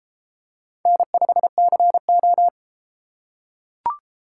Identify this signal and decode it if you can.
Morse code, heard as a keyed tone, about 32 wpm, 700 Hz. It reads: N5CO